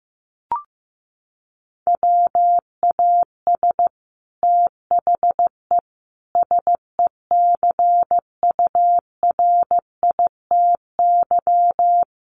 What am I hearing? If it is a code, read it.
Morse code, heard as a keyed tone, about 15 wpm, 700 Hz. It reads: WAS THE SECURITY
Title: Sentences from Top 1400 Words 15wpm